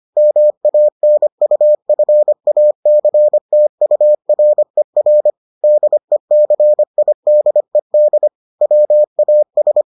Morse code, heard as a keyed tone, about 25 wpm, 600 Hz. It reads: MANUFACTURER DECIDED WAS